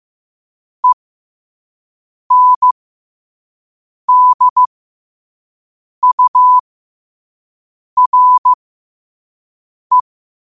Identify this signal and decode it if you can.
Morse code, heard as a keyed tone, about 15 wpm, spaced as for 5 wpm, 1000 Hz. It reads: ENDURE
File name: qotdfiles / 5wpm